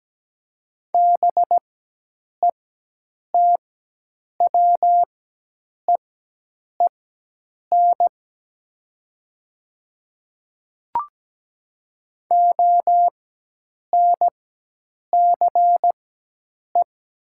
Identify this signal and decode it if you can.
Morse code, heard as a keyed tone, about 17 wpm, spaced as for 8 wpm, 700 Hz. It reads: BETWEEN ONCE
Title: 4x Character Spacing - Top 300 Words - 17wpm